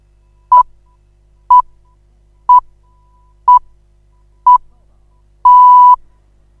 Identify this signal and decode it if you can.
Radio time signal (time pips). It is heard from 0.5 to 5.9 s.